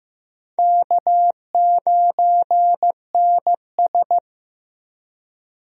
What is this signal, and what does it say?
Morse code, heard as a keyed tone, about 15 wpm, 700 Hz. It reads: K9NS
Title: USA Call Signs - 4 Characters - Repeated 3x - 15wpm